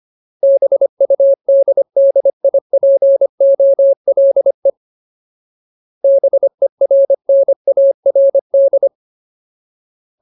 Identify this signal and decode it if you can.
Morse code, heard as a keyed tone, about 25 wpm, 550 Hz. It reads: BUDDIPOLE BERNARD
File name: Ultimate Word List / 25WPM Ultimate